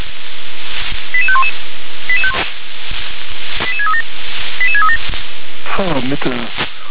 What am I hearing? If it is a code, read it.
Five-tone selective call (ZVEI), heard as quick runs of five tones, tones of about 80 ms, with speech. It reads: 88511 88637 88637